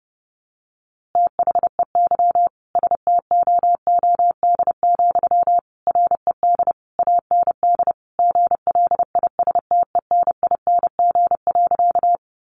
Morse code, heard as a keyed tone, about 30 wpm, 700 Hz. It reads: THEY STOOD, RED AND GLISTENING.
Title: A Sound Of Thunder 30wpm